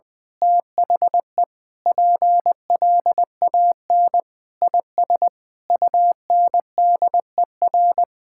Morse code, heard as a keyed tone, about 20 wpm, 700 Hz. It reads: THE PLAN IS UNDER
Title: Sentences with 9-Word Phrases - Spoken Beginning - 20wpm